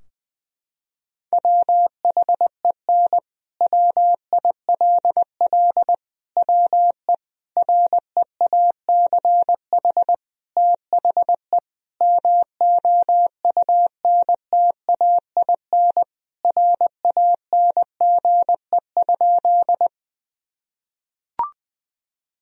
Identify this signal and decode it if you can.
Morse code, heard as a keyed tone, about 20 wpm, 700 Hz. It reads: WHEN WILL WE REACH THE MOUNTAIN RANGE?
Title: Sentences from Top 1000 Words - Review Part 4 - 20wpm